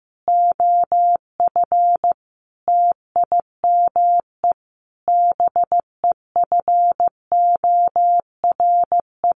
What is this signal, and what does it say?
Morse code, heard as a keyed tone, about 14 wpm, 700 Hz. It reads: OF TIME BEFORE
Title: Sets of 5 Words Encore 15wpm